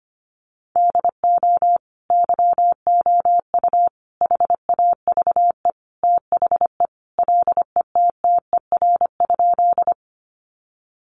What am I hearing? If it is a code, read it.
Morse code, heard as a keyed tone, about 25 wpm, 700 Hz. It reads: DO YOU HAVE THE LETTER?
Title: Sentences from Top 300 Words 25wpm